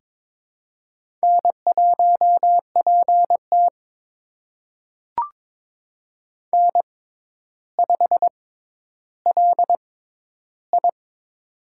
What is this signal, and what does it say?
Morse code, heard as a keyed tone, about 22 wpm, spaced as for 11 wpm, 700 Hz. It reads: N1PT N5LI
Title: USA Call Signs - 6x Character Spacing - 22wpm